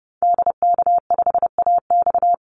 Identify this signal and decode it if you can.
Morse code, heard as a keyed tone, about 30 wpm, 700 Hz. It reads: DK5AX
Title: Call Signs Repeated 3x - 30wpm